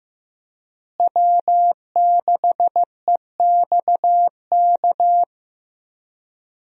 Morse code, heard as a keyed tone, about 15 wpm, 700 Hz. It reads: W6EXK
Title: USA Call Signs - 5 Characters - Repeated 2x - 15wpm